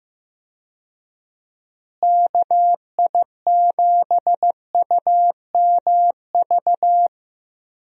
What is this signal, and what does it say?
Morse code, heard as a keyed tone, about 16 wpm, 700 Hz. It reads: KI7UMV
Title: USA Call Signs - 6 Characters - Repeated 3x - 15wpm